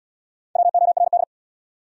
Morse code, heard as a keyed tone, about 68 wpm, 690 Hz. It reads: WORD